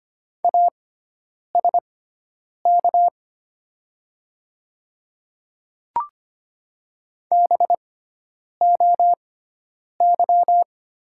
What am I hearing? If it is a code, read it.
Morse code, heard as a keyed tone, about 25 wpm, spaced as for 9 wpm, 700 Hz. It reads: ASK BOY
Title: 6x Character Spacing - Top 200 Words - 25wpm